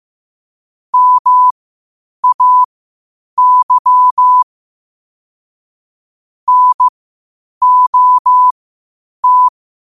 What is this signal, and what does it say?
Morse code, heard as a keyed tone, about 15 wpm, spaced as for 8 wpm, 1000 Hz. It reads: MAY NOT
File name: qotdfiles / 8wpm